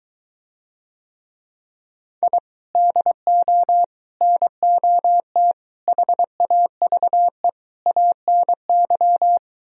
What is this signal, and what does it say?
Morse code, heard as a keyed tone, about 23 wpm, 700 Hz. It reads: I DO NOT HAVE ANY